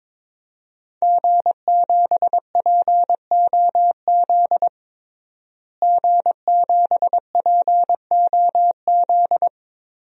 Morse code, heard as a keyed tone, about 22 wpm, 700 Hz. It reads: G7POZ G7POZ
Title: DXCC-100 Call Signs - 5 Characters - Repeated 3x - 22wpm